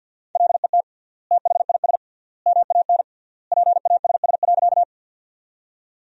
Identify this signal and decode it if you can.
Morse code, heard as a keyed tone, about 50 wpm, 700 Hz. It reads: LET THIS MAN PASS.
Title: 3x Word-Spacing - Sentences from Top 400 Words - 50wpm